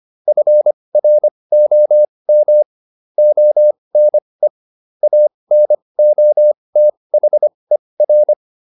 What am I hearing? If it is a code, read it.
Morse code, heard as a keyed tone, about 25 wpm, spaced as for 20 wpm, 600 Hz. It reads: FROM ONE ANOTHER